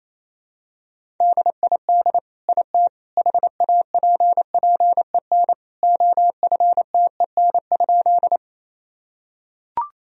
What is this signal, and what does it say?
Morse code, heard as a keyed tone, about 28 wpm, 700 Hz. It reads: DID IT HAPPEN OFTEN?
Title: Sentences from Top 1000 Words - Review Part 1 - 28wpm